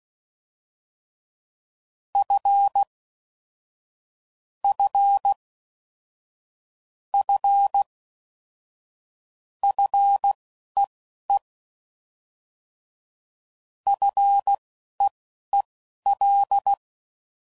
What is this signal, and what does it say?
Morse code, heard as a keyed tone, about 16 wpm, spaced as for 11 wpm, 800 Hz. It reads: F F F FEE FEEL
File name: Basic Sets / QST de WA7RSO / Combnations